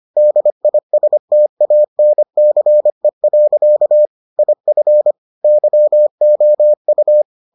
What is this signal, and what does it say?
Morse code, heard as a keyed tone, about 25 wpm, 600 Hz. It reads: DISTANCE. IF YOU